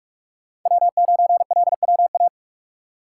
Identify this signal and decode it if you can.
Morse code, heard as a keyed tone, about 44 wpm, 700 Hz. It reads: W9LWA
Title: USA Call Signs - 5 Characters - 6x Character Spacing - 45wpm